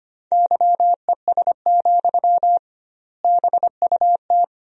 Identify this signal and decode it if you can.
Morse code, heard as a keyed tone, about 25 wpm, 700 Hz. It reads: YES, BUT
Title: 2x Word-Spacing - Sentences from Top 500 Words - 25wpm